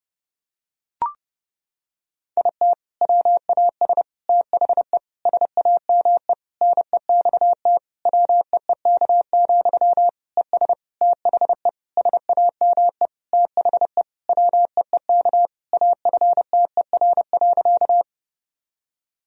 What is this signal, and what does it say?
Morse code, heard as a keyed tone, about 30 wpm, 700 Hz. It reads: IT WAS THE SAME NEXT WEEK, ES THE SAME THE WEEK AFTER.
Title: Sentences from Top 500 Words and Abbreviations 30wpm